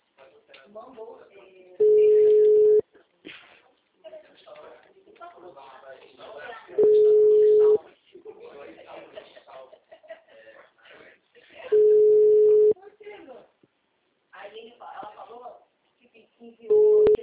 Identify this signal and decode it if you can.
Ringback tone (Europe, 425 Hz), with speech.
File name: out- -2016- - - .79976